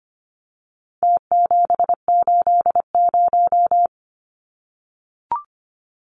Morse code, heard as a keyed tone, about 25 wpm, 700 Hz. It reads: T780
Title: Numbers - Contest Style - 4-Digit 25wpm